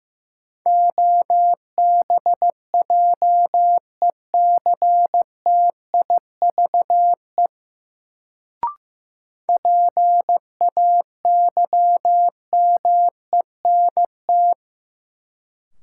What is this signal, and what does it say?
Morse code, heard as a keyed tone, about 15 wpm, 700 Hz. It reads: OBJECTIVE PAYMENT
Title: Sentences from Top 1200 Words - New Word Drill Alt - 15wpm